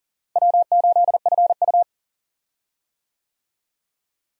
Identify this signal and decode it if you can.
Morse code, heard as a keyed tone, about 40 wpm, 700 Hz. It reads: W8FU
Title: USA Call Signs - 4 Characters - Repeated 2x - 40wpm